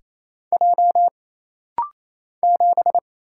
Morse code, heard as a keyed tone, about 28 wpm, 700 Hz. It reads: J 7
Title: Single Letter-Number - Rapid-Fire 28wpm